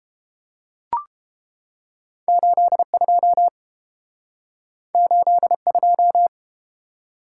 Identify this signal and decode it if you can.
Morse code, heard as a keyed tone, about 31 wpm, 700 Hz. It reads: 82 82